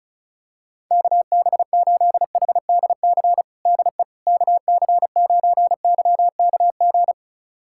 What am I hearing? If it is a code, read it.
Morse code, heard as a keyed tone, about 34 wpm, 700 Hz. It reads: KB8HDC DE KC9YKG
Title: USA Callsign DE Callsign - 6 Characters - 35wpm